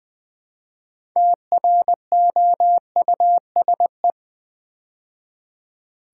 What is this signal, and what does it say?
Morse code, heard as a keyed tone, about 20 wpm, 700 Hz. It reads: TROUSE
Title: Build A Word - 2 Letters At A Time - 8-Letter Words - 20wpm